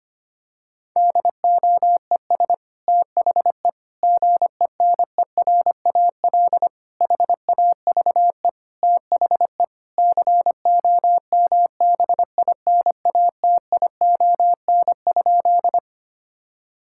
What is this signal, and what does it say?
Morse code, heard as a keyed tone, about 25 wpm, 700 Hz. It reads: DOES THE GENERAL HAVE THE COMBINATION?
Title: Sentences from Top 1000 Words - Review Part 6 - 25wpm